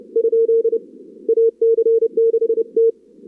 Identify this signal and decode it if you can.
Morse code, heard as a keyed tone, about 28 wpm, 440 Hz. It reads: ? AC6T
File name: CWSprint / Feb2022 / k5tr-feb-na-sprint-cw-k5ot -0100